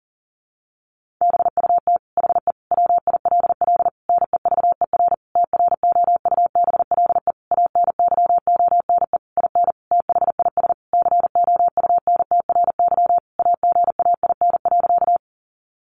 Morse code, heard as a keyed tone, about 40 wpm, 700 Hz. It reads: BUT HE WILL NEVER TROUBLE ANYONE IN THIS COUNTRY AGAIN.
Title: Sentences from Top 1200 Words 40wpm